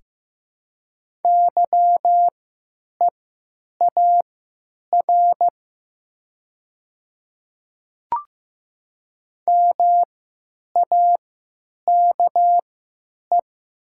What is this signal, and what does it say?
Morse code, heard as a keyed tone, about 15 wpm, spaced as for 8 wpm, 700 Hz. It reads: YEAR MAKE
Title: 3x Character Spacing - Top 500 Words - 15wpm